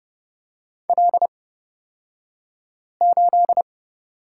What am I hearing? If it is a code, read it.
Morse code, heard as a keyed tone, about 30 wpm, spaced as for 5 wpm, 700 Hz. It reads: L8
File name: Lesson 38d - Letter q - Two Characters 30wpm